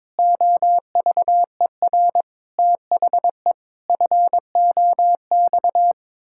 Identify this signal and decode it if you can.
Morse code, heard as a keyed tone, about 22 wpm, 700 Hz. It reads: OVER THE FOX
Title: Pangram Challenge v2 - 22wpm